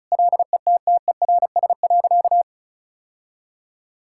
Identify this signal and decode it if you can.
Morse code, heard as a keyed tone, about 34 wpm, 700 Hz. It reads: LETTERS.
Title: Sentences from Top 1000 Words - Review Part 6 - 35wpm